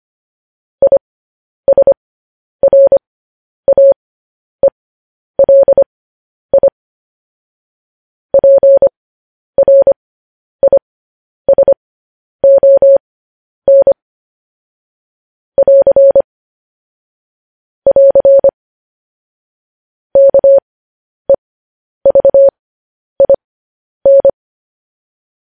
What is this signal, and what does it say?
Morse code, heard as a keyed tone, about 25 wpm, spaced as for 10 wpm, 550 Hz. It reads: ISRAELI PRISON <AR> <AR> KEVIN